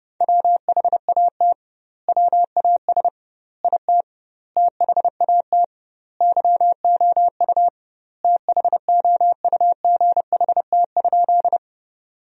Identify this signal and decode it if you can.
Morse code, heard as a keyed tone, about 32 wpm, 700 Hz. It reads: WHAT WAS IT THAT YOU THOUGHT?
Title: 2x Word-Spacing - Sentences from Top 300 Words - 30wpm